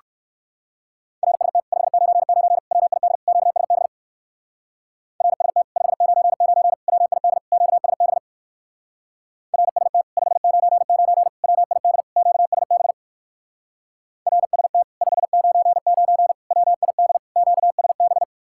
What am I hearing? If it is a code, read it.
Morse code, heard as a keyed tone, about 56 wpm, 700 Hz. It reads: RST 599 WID QSB RST 599 WID QSB RST 599 WID QSB RST 599 WID QSB
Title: Speed-Racing Common QSO Phrases - 45wpm